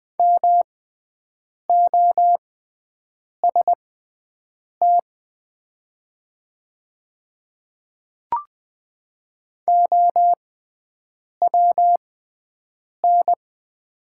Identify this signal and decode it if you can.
Morse code, heard as a keyed tone, about 21 wpm, spaced as for 7 wpm, 700 Hz. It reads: MOST OWN
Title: 6x Character Spacing - Top 300 Words - 20wpm